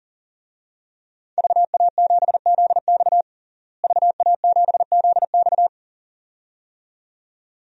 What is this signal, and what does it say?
Morse code, heard as a keyed tone, about 37 wpm, 700 Hz. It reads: VA7ZX VA7ZX
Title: DXCC-100 Call Signs - 5 Characters - Repeated 2x - 40wpm